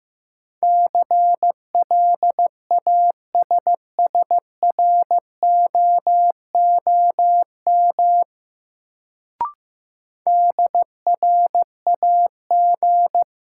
Morse code, heard as a keyed tone, about 15 wpm, 700 Hz. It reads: CLASSROOM DRAG
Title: Sentences from Top 1500 Words - New Word Drill - 15wpm